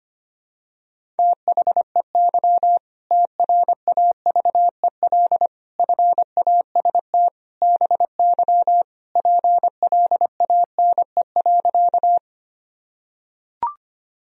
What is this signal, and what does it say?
Morse code, heard as a keyed tone, about 25 wpm, 700 Hz. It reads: THEY TRAVEL FAST BY PLANE.
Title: Sentences from Top 1000 Words - Review Part 3 - 25wpm